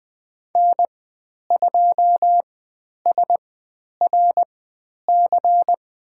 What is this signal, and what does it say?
Morse code, heard as a keyed tone, about 20 wpm, spaced as for 10 wpm, 700 Hz. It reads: N2SRC
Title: 102 Lesson 17g - Letter u - Call Signs 20wpm